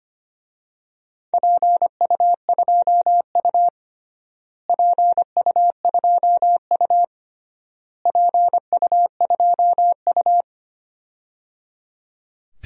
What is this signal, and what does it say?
Morse code, heard as a keyed tone, about 25 wpm, 700 Hz. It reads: PU2U PU2U PU2U
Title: DXCC-100 Call Signs - 4 Characters - Repeated 3x - 25wpm